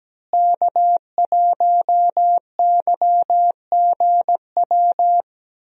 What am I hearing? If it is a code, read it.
Morse code, heard as a keyed tone, about 17 wpm, 700 Hz. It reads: K1YGW
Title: USA Call Signs - 5 Characters - Repeated 3x - 17wpm